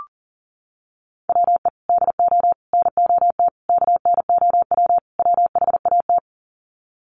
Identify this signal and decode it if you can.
Morse code, heard as a keyed tone, about 40 wpm, 700 Hz. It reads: WE DO NOT KNOW WHAT
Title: Sets of 5 Words 40wpm